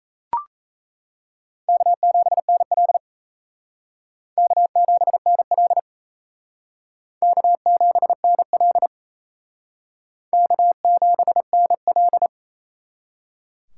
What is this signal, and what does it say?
Morse code, heard as a keyed tone, about 37 wpm, 700 Hz. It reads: K7NL K7NL K7NL K7NL